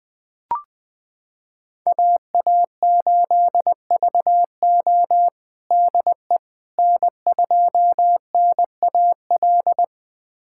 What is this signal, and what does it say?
Morse code, heard as a keyed tone, about 20 wpm, 700 Hz. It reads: AA8VO DE N2NAL
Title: USA Callsign DE Callsign - 5 Characters - 20wpm